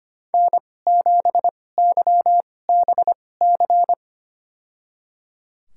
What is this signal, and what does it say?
Morse code, heard as a keyed tone, about 25 wpm, spaced as for 18 wpm, 700 Hz. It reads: N7YBC
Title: USA Call Signs - 5 Characters - 2x Character Spacing - 25wpm